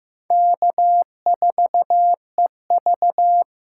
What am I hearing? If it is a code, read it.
Morse code, heard as a keyed tone, about 15 wpm, 700 Hz. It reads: K4EV
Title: USA Call Signs - 4 Characters - 3x Character Spacing - 15wpm